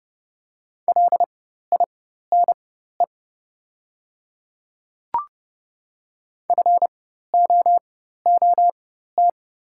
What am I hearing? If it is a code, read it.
Morse code, heard as a keyed tone, about 30 wpm, spaced as for 14 wpm, 700 Hz. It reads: LINE FOOT
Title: 4x Character Spacing - Top 500 Words - 30wpm